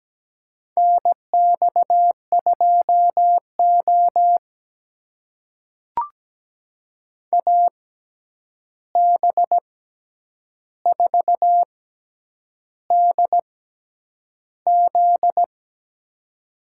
Morse code, heard as a keyed tone, about 17 wpm, spaced as for 8 wpm, 700 Hz. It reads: NX2O AB4DZ
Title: USA Call Signs - 6x Character Spacing - 17wpm